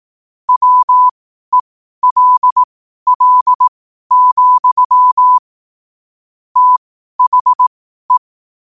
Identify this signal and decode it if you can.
Morse code, heard as a keyed tone, about 18 wpm, spaced as for 12 wpm, 1000 Hz. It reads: WELL, THE
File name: qotdfiles / 12wpm